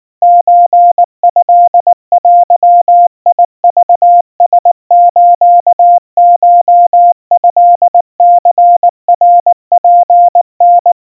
Morse code (Japanese Wabun code), heard as a keyed tone, about 19 wpm, 700 Hz. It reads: ソトデクラスコトニナツタ
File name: kawa / Morse sample / kamyu 70 cpm 700 Hz Crispy